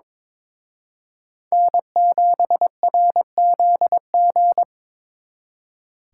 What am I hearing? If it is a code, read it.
Morse code, heard as a keyed tone, about 25 wpm, 700 Hz. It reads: N7RZG
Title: Speed Racing - USA 1x3 Call Signs 22wpm